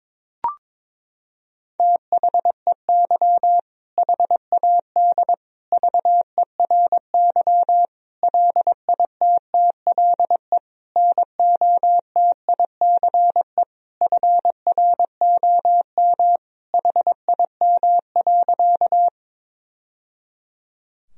Morse code, heard as a keyed tone, about 21 wpm, 700 Hz. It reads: THEY HAD VERY LITTLE NOTICE FROM HIM.
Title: Sentences from Top 1000 Words - Review Part 2 - 22wpm